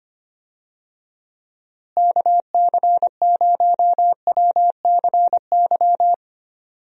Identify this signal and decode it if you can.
Morse code, heard as a keyed tone, about 25 wpm, 700 Hz. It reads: KC0WCY
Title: USA Call Signs - 6 Characters - Repeated 3x - 25wpm